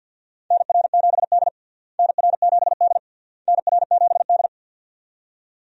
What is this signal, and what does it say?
Morse code, heard as a keyed tone, about 46 wpm, 700 Hz. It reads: NR7D NR7D NR7D
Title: USA Call Signs - 4 Characters - Repeated 3x - 50wpm